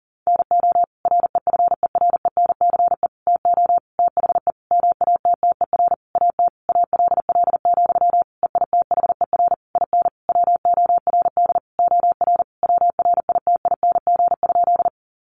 Morse code, heard as a keyed tone, about 40 wpm, 700 Hz. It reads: NO REFERENCE TO THE MATTER AT ALL, EITHER IN WORD OR WRITING?
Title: Sentences from Top 1200 Words 40wpm